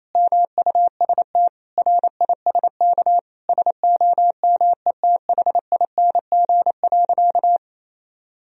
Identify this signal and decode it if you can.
Morse code, heard as a keyed tone, about 28 wpm, 700 Hz. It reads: MUST RISK SOMETHING.
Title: Sentences from Top 1000 Words - Review Part 4 - 28wpm